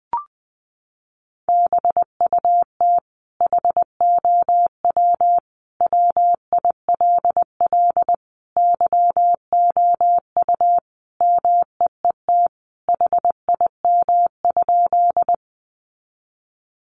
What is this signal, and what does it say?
Morse code, heard as a keyed tone, about 20 wpm, 700 Hz. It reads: BUT HOW WILL YOU MEET HIM?
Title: Sentences from Top 1100 Words 20wpm